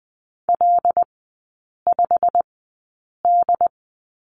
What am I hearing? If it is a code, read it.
Morse code, heard as a keyed tone, about 20 wpm, spaced as for 8 wpm, 700 Hz. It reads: L5D
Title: 138 Lesson 23e - Letter f - Three Characters 20wpm